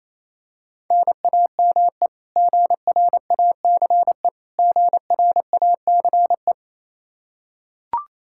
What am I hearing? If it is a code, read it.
Morse code, heard as a keyed tone, about 27 wpm, 700 Hz. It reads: NAME GRACE GRACE
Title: Practice QSO - Exchange 1 - Beginning - 28wpm